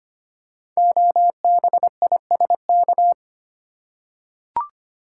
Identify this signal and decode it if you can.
Morse code, heard as a keyed tone, about 25 wpm, 700 Hz. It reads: OBISK
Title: Code Groups - 5 Letters - 25wpm